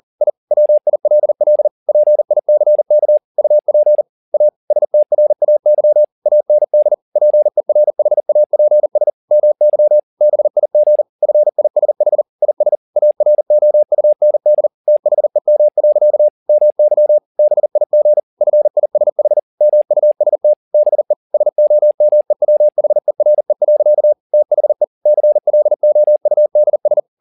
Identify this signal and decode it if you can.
Morse code, heard as a keyed tone, about 40 wpm, 600 Hz. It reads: I WILL PICK UP A STRAY AND PERHAPS MY BIG FISH IS AROUND THEM. MY BIG FISH MUST BE SOMEWHERE. THE CLOUDS